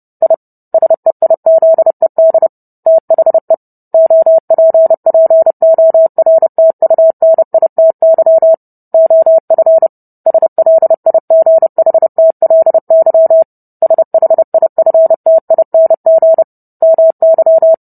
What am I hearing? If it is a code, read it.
Morse code, heard as a keyed tone, about 30 wpm, 650 Hz. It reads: I SEIZED THE OPPORTUNITY OF SLIGHTLY SHIFTING MY